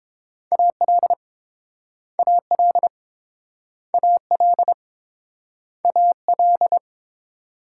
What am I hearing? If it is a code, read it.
Morse code, heard as a keyed tone, about 30 wpm, 700 Hz. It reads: AL AL AL AL